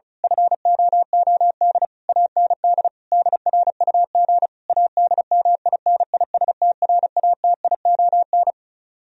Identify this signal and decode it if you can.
Morse code, heard as a keyed tone, about 32 wpm, 700 Hz. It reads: FOOD AND DRUG ADMINISTRATION
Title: Speed-Racing - Sets of 5 Words 35wpm